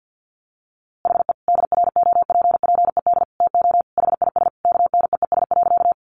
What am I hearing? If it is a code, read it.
Morse code, heard as a keyed tone, about 48 wpm, 700 Hz. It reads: HE DROPPED TO HIS KNEES.
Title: A Sound Of Thunder 50wpm